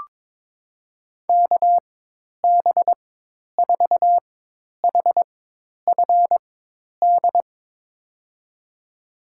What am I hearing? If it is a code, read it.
Morse code, heard as a keyed tone, about 21 wpm, spaced as for 10 wpm, 700 Hz. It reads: KB4HFD
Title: USA Call Signs - 6 Characters - 4x Character Spacing - 22wpm